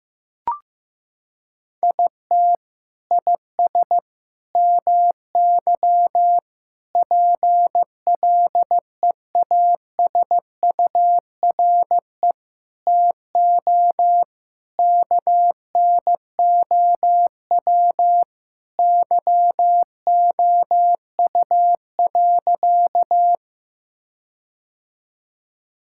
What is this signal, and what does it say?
Morse code, heard as a keyed tone, about 15 wpm, 700 Hz. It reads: IT IS MY PLEASURE TO KNOW YOU.
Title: Sentences from Top 1600 Words 15wpm